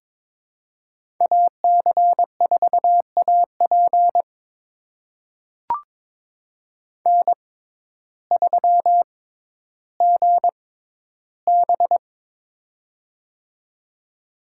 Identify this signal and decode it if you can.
Morse code, heard as a keyed tone, about 22 wpm, spaced as for 12 wpm, 700 Hz. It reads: AC4AP N3GB
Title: USA Call Signs - 6x Character Spacing - 22wpm